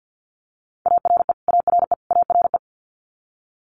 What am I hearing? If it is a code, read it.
Morse code, heard as a keyed tone, about 50 wpm, 700 Hz. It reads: ARE ARE ARE
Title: Top 500 Words Repeated 50wpm